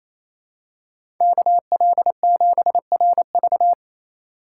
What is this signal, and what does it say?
Morse code, heard as a keyed tone, about 28 wpm, 700 Hz. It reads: KL7RV
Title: USA Call Signs - 3x Character Spacing - 28wpm